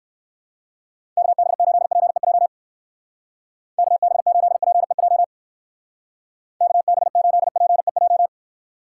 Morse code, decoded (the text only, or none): KB8PEJ KB8PEJ KB8PEJ